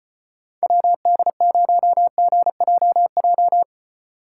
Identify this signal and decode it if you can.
Morse code, heard as a keyed tone, about 32 wpm, 700 Hz. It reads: WD0GJJ